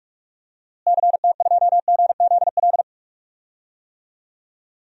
Morse code, heard as a keyed tone, about 46 wpm, 700 Hz. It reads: CT2GZL
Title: DXCC-100 Call Signs - 6 Characters - Repeated 2x - 45wpm